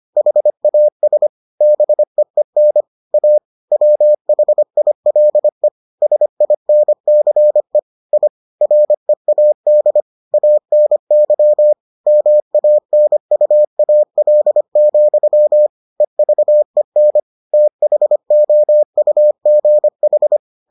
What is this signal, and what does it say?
Morse code, heard as a keyed tone, about 25 wpm, 600 Hz. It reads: HAS BEEN A WHILE SINCE I READ ANY MANUAL, EVEN THOUGH